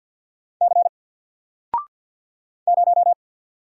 Morse code, heard as a keyed tone, about 48 wpm, 700 Hz. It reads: C 0
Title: Single Letter-Number Alt 50wpm